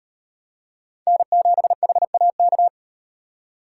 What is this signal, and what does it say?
Morse code, heard as a keyed tone, about 38 wpm, 700 Hz. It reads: N7HAK